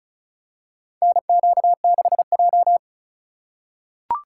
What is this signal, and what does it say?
Morse code, heard as a keyed tone, about 36 wpm, 700 Hz. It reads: NQ6J
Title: USA Call Signs - 4 Characters - 3x Character Spacing - 35wpm